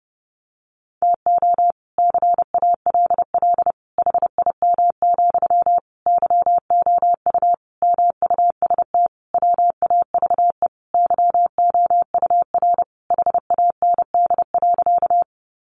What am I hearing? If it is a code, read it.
Morse code, heard as a keyed tone, about 30 wpm, 700 Hz. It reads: TO CALL HIM, YOU MUST WAVE YOUR HAND.
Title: Sentences from Top 1400 Words 30wpm